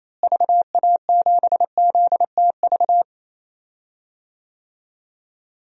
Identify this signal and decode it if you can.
Morse code, heard as a keyed tone, about 28 wpm, 700 Hz. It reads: VA7ZTV
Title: DXCC-100 Call Signs - 6 Characters - Repeated 2x - 28wpm